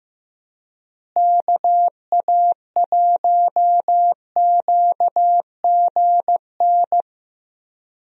Morse code, heard as a keyed tone, about 15 wpm, 700 Hz. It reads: KA1QGN
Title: USA Call Signs - 6 Characters - Repeated 3x - 15wpm